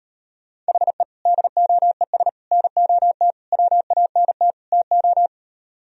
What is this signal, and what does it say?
Morse code, heard as a keyed tone, about 38 wpm, 700 Hz. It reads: HE DOES NOT WANT TO